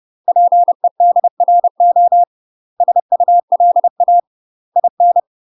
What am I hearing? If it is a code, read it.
Morse code, heard as a keyed tone, about 30 wpm, 700 Hz. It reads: PEDRO SULA IN
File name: UK MASTER QSOs TRAINING / 30WPM / UK MASTER QSOs 0033